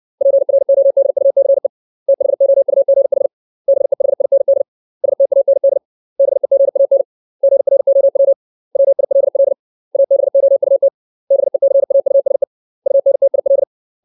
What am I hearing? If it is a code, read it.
Morse code, heard as a keyed tone, about 60 wpm, 550 Hz. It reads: PRODUCE THOUGH BEHIND STAND BEGAN GROW WILL ABOUT BECAUSE UNTIL
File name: 632 words / 632 Words 60WPM